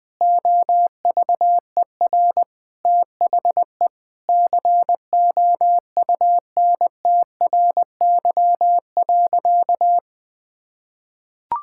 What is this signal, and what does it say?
Morse code, heard as a keyed tone, about 20 wpm, 700 Hz. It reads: OVER THE COUNTRY.
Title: Sentences with 4-Word Phrases - 20wpm